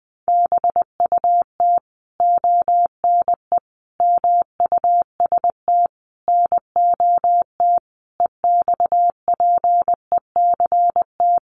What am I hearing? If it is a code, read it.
Morse code, heard as a keyed tone, about 20 wpm, 700 Hz. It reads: BUT ONE MUST NOT EXPECT
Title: Sentences from Top 1100 Words 20wpm